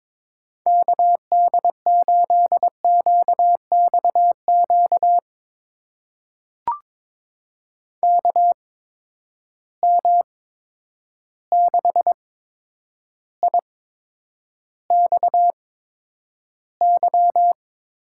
Morse code, heard as a keyed tone, about 22 wpm, spaced as for 9 wpm, 700 Hz. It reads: KD8QXQ KM6IXY
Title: USA Call Signs - 6 Characters - 8x Character Spacing - 22wpm